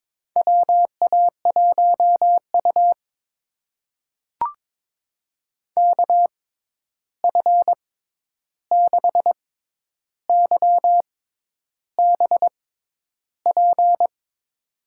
Morse code, heard as a keyed tone, about 22 wpm, spaced as for 10 wpm, 700 Hz. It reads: WA1U KF6YBP
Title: USA Call Signs - 6x Character Spacing - 22wpm